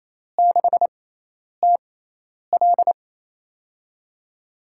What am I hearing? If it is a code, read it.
Morse code, heard as a keyed tone, about 28 wpm, spaced as for 10 wpm, 700 Hz. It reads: 6TL
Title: USA Call Signs - 5 Characters - 6x Character Spacing - 28wpm